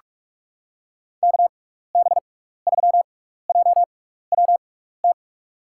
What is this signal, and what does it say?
Morse code, heard as a keyed tone, about 44 wpm, spaced as for 15 wpm, 700 Hz. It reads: KB3JWT
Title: USA Call Signs - 6 Characters - 6x Character Spacing - 45wpm